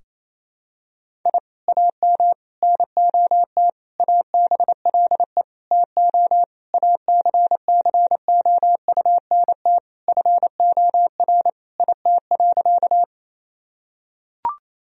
Morse code, heard as a keyed tone, about 28 wpm, 700 Hz. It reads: I AM NOT ABLE TO ACCOUNT FOR IT.
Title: Sentences from Top 1000 Words - Review Part 2 - 28wpm